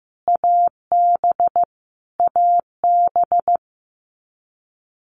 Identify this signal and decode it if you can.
Morse code, heard as a keyed tone, about 15 wpm, 700 Hz. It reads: AB AB
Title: Canadian Provinces and Territories - Repeated 3x - 15wpm